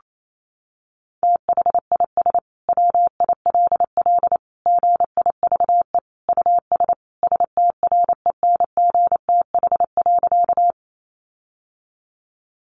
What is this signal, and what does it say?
Morse code, heard as a keyed tone, about 28 wpm, 700 Hz. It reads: THIS WILL GIVE US STRENGTH.
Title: Sentences from Top 1300 Words 28wpm